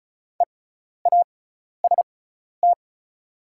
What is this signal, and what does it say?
Morse code, heard as a keyed tone, about 34 wpm, spaced as for 12 wpm, 690 Hz. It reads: EAST
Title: 6x Character Spacing - Top 500 Words - 35wpm